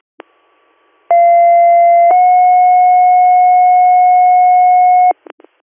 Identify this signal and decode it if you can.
Two-tone page, heard as a one-second tone, then a three-second tone, about 690 then 730 Hz. It is heard from 1.1 to 5.1 s.